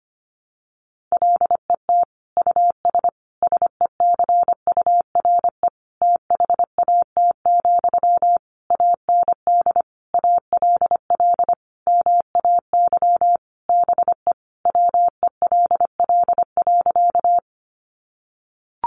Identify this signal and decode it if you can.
Morse code, heard as a keyed tone, about 25 wpm, 700 Hz. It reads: LET US SECURE THAT, AND ALL MAY BE WELL.
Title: Sentences from Top 1200 Words 25wpm